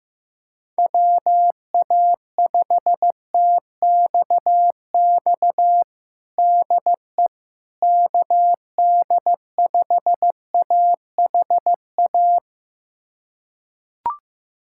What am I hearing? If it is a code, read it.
Morse code, heard as a keyed tone, about 15 wpm, 700 Hz. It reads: WA5TXX DE KD5AHA
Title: USA Callsign DE Callsign - 6 Characters - 15wpm